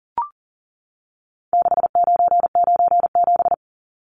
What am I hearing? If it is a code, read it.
Morse code, heard as a keyed tone, about 40 wpm, 700 Hz. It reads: 6997
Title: Numbers - 4-Digit 40wpm